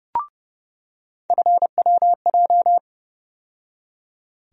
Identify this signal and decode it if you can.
Morse code, heard as a keyed tone, about 29 wpm, 700 Hz. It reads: FWJ
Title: USA Call Sign Postfixes - 3 Letters 30wpm